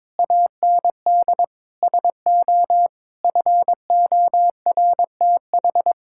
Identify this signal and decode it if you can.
Morse code, heard as a keyed tone, about 22 wpm, 700 Hz. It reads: AND SO FORTH
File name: practice / 22wpm / Sets of 6 Words - 22wpm